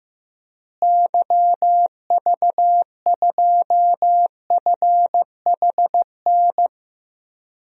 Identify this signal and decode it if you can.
Morse code, heard as a keyed tone, about 15 wpm, 700 Hz. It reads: YV2FHN
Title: DXCC-100 Call Signs - 6 Characters - Repeated 3x - 15wpm